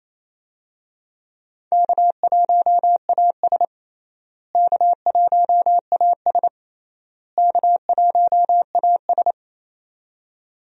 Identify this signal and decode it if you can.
Morse code, heard as a keyed tone, about 28 wpm, 700 Hz. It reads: K1AS K1AS K1AS
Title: USA Call Signs - 4 Characters - Repeated 3x - 28wpm